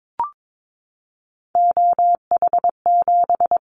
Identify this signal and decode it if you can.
Morse code, heard as a keyed tone, about 22 wpm, 700 Hz. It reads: OH7
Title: Callsign Prefixes for Top 25 DXCC - 22wpm